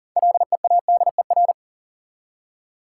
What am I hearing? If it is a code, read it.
Morse code, heard as a keyed tone, about 40 wpm, 700 Hz. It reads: LEADER
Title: Six Letter Words - 40wpm